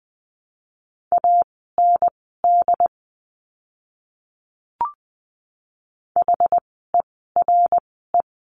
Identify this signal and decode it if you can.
Morse code, heard as a keyed tone, about 20 wpm, spaced as for 15 wpm, 700 Hz. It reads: AND HERE
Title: 192 Lesson 31f - Letter v - Words 20wpm